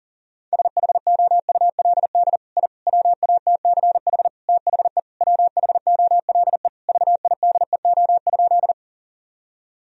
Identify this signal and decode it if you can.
Morse code, heard as a keyed tone, about 40 wpm, 700 Hz. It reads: SHOULD I WATCH THE WHOLE VIDEO?
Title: Sentences from Top 1000 Words - Review Part 4 - 40wpm